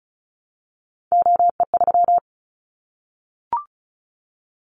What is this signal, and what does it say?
Morse code, heard as a keyed tone, about 34 wpm, 700 Hz. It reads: OE3
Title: Callsign Prefixes for Top 100 DXCC - 35wpm